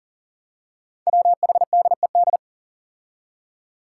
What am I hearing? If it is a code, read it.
Morse code, heard as a keyed tone, about 38 wpm, 700 Hz. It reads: WHDED
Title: Five Challenging Letter-Number Combinations 40wpm